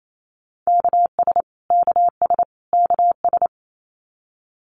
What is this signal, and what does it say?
Morse code, heard as a keyed tone, about 28 wpm, 700 Hz. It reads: KS KS KS
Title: US States Repeated 3x 28wpm